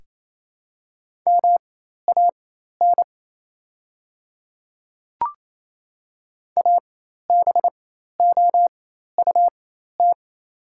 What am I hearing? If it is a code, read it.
Morse code, heard as a keyed tone, about 28 wpm, spaced as for 13 wpm, 700 Hz. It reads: MAN ABOUT
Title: 4x Character Spacing - Top 500 Words - 28wpm